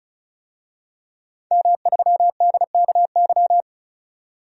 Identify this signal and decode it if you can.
Morse code, heard as a keyed tone, about 36 wpm, 700 Hz. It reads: M3DKY
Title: DXCC-100 Call Signs - 5 Characters - 35wpm